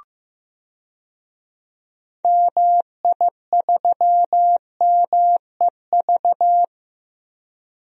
Morse code, heard as a keyed tone, about 15 wpm, 700 Hz. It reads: MI3MEV
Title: DXCC-100 Call Signs - 6 Characters - Repeated 2x - 15wpm